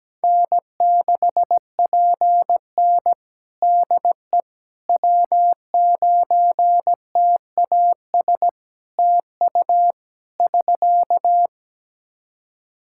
Morse code, heard as a keyed tone, about 17 wpm, 700 Hz. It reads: N6PN DE W9TAS TU <SK>
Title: Practice QSO - Exchange 4 - Ending - 17wpm